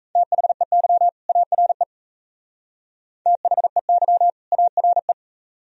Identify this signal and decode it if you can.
Morse code, heard as a keyed tone, about 34 wpm, 700 Hz. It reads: THEY ARE THEY ARE
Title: Speed-Racing - Sets of 2 Words 28wpm